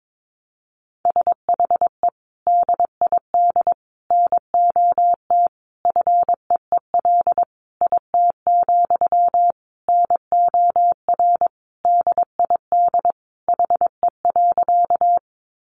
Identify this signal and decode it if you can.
Morse code, heard as a keyed tone, about 22 wpm, 700 Hz. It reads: SHE DID NOT FEEL IT, NOR DID HE.
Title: Sentences from Top 1500 Words 22wpm